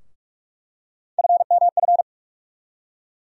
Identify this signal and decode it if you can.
Morse code, heard as a keyed tone, about 43 wpm, 700 Hz. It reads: FMF
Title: USA Call Sign Postfixes - 3 Letters 45wpm